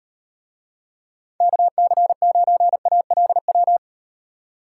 Morse code, heard as a keyed tone, about 38 wpm, 700 Hz. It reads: KC9ALW